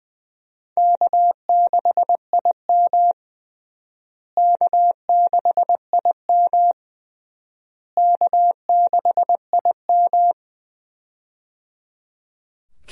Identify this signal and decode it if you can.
Morse code, heard as a keyed tone, about 19 wpm, 700 Hz. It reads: K6IM K6IM K6IM
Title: USA Call Signs - 4 Characters - Repeated 3x - 20wpm